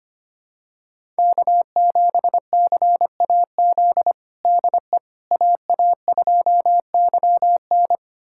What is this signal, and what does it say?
Morse code, heard as a keyed tone, about 25 wpm, 700 Hz. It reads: K7CAZ DE AA2YN
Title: USA Callsign DE Callsign - 5 Characters - 25wpm